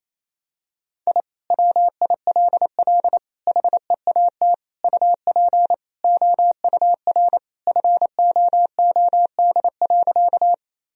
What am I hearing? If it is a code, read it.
Morse code, heard as a keyed tone, about 28 wpm, 700 Hz. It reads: I WILL HEAT UP OUR FOOD.
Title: Sentences from Top 1000 Words - Review Part 3 - 28wpm